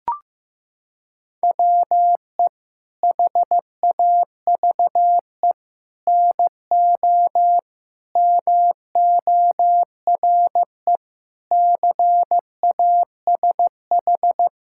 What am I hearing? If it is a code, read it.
Morse code, heard as a keyed tone, about 15 wpm, 700 Hz. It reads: WE HAVE NO MORE CASH
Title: Sentences from Top 1000 Words - Review Part 5 - 15wpm